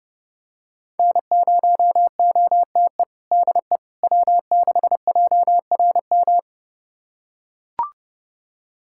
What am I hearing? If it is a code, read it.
Morse code, heard as a keyed tone, about 30 wpm, 700 Hz. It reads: N0OTE DE W6JRM
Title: USA Callsign DE Callsign - 5 Characters - 30wpm